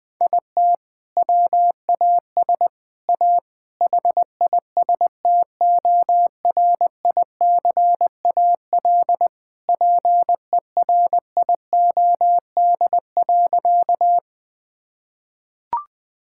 Morse code, heard as a keyed tone, about 20 wpm, 700 Hz. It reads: IT WAS A HISTORICAL PERIOD.
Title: Sentences from Top 1000 Words - Review Part 5 - 20wpm